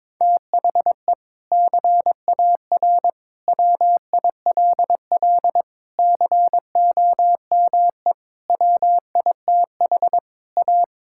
Morse code, heard as a keyed tone, about 22 wpm, 700 Hz. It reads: THE CAR WILL COME WITH A
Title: Sentences from Top 1600 Words 22wpm